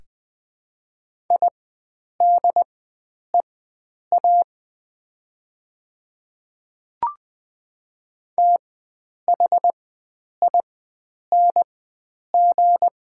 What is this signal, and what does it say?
Morse code, heard as a keyed tone, about 20 wpm, spaced as for 9 wpm, 700 Hz. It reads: IDEA THING
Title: 4x Character Spacing - Top 300 Words - 20wpm